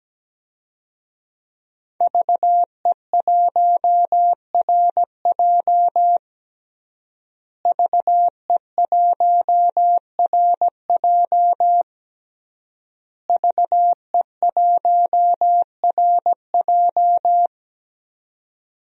Morse code, heard as a keyed tone, about 17 wpm, 700 Hz. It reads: VE1RJ VE1RJ VE1RJ
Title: DXCC-100 Call Signs - 5 Characters - Repeated 3x - 17wpm